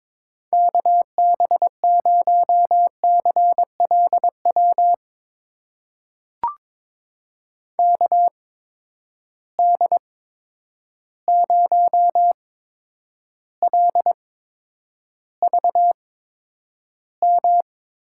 Morse code, heard as a keyed tone, about 22 wpm, spaced as for 9 wpm, 700 Hz. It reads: KB0CLW KD0LVM
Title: USA Call Signs - 6 Characters - 8x Character Spacing - 22wpm